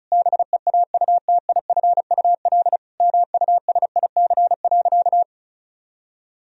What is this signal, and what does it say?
Morse code, heard as a keyed tone, about 34 wpm, 700 Hz. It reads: BEAUTIFUL MUSIC.
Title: Sentences from Top 1000 Words - Review Part 1 - 35wpm